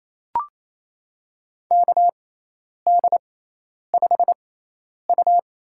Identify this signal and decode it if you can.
Morse code, heard as a keyed tone, about 27 wpm, spaced as for 10 wpm, 700 Hz. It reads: KD5U
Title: USA Call Signs - 6 Characters - 6x Character Spacing - 28wpm